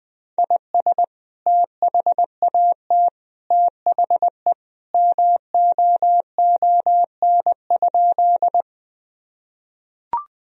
Morse code, heard as a keyed tone, about 20 wpm, 700 Hz. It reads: IS THAT THE MOON?
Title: Sentences from Top 1000 Words - Review Part 3 - 20wpm